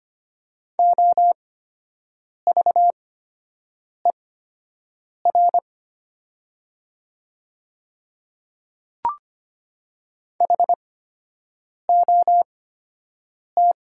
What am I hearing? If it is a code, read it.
Morse code, heard as a keyed tone, about 25 wpm, spaced as for 7 wpm, 700 Hz. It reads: OVER HOT
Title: 8x Character Spacing - Top 100 Words - 25wpm